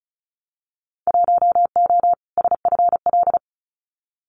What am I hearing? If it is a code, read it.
Morse code, heard as a keyed tone, about 36 wpm, 700 Hz. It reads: 1O SFL
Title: Winter Field Day Exchanges 35wpm